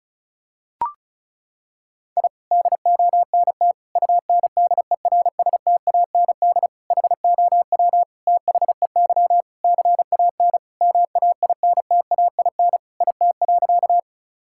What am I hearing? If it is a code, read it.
Morse code, heard as a keyed tone, about 34 wpm, 700 Hz. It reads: I DONT UNDERSTAND HOW THEY CAN MAINTAIN IT.
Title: Sentences from Top 1000 Words - Review Part 5 - 35wpm